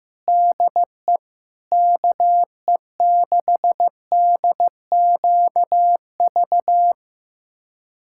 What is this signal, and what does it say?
Morse code, heard as a keyed tone, about 15 wpm, 700 Hz. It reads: DE KE6DQV
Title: USA Callsign DE Callsign - 6 Characters - 15wpm